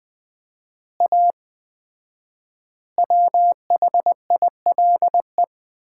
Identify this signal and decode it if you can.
Morse code, heard as a keyed tone, about 21 wpm, 700 Hz. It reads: A WHILE
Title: 4x Word-Spacing - Sets of 3 Words - 20wpm